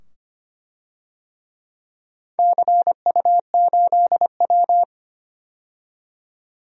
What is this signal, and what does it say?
Morse code, heard as a keyed tone, about 25 wpm, 700 Hz. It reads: CU8W
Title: DXCC-100 Call Signs - 4 Characters - 25wpm